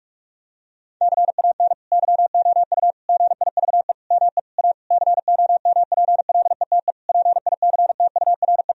Morse code, heard as a keyed tone, about 41 wpm, 700 Hz. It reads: CAN YOU GIVE ME A COMPLETE PICTURE
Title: Sentences from Top 1000 Words - Review Part 2 - 45wpm